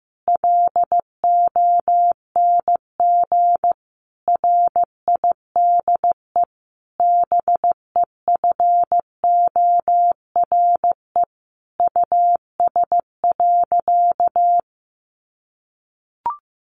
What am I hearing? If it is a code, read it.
Morse code, heard as a keyed tone, about 15 wpm, 700 Hz. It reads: LONG RIDE BEFORE US.
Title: Sentences from Top 1200 Words 15wpm